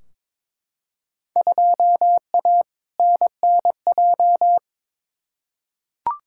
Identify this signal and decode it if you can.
Morse code, heard as a keyed tone, about 22 wpm, 700 Hz. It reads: 2A NNJ
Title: Field Day Exchanges - 22wpm